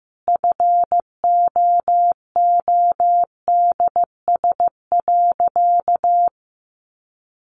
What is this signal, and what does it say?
Morse code, heard as a keyed tone, about 15 wpm, 700 Hz. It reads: FOODS.
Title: Easy Phrases 15wpm